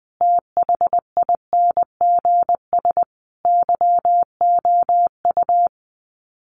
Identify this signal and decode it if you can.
Morse code, heard as a keyed tone, about 20 wpm, 700 Hz. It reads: THINGS YOU
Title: Sets of 5 Words Encore2 20wpm